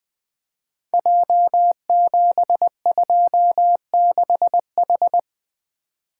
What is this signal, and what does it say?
Morse code, heard as a keyed tone, about 20 wpm, 700 Hz. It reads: J726H
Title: Code Groups - 5 Letter:Numbers - 20wpm